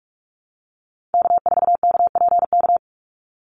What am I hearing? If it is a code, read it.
Morse code, heard as a keyed tone, about 44 wpm, 700 Hz. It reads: K4KPK
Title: NA SOTA Activator Callsigns 45wpm